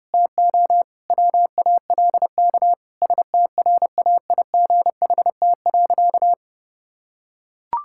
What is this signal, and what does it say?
Morse code, heard as a keyed tone, about 32 wpm, 700 Hz. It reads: TO WALK STRAIGHT.
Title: Sentences from Top 1000 Words - Review Part 6 - 30wpm